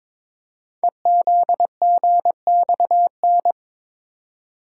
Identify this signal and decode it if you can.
Morse code, heard as a keyed tone, about 22 wpm, 700 Hz. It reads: EZGXN
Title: Code Groups - 5 Letter:Numbers - 22wpm